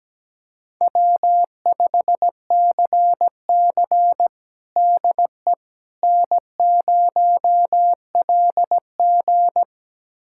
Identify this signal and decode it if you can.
Morse code, heard as a keyed tone, about 17 wpm, 700 Hz. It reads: W5CC DE N0LG
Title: USA Callsign DE Callsign - 4 Characters - 17wpm